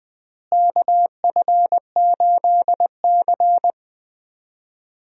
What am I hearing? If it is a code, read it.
Morse code, heard as a keyed tone, about 21 wpm, 700 Hz. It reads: KF8C
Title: USA Call Signs - 4 Characters - Repeated 2x - 20wpm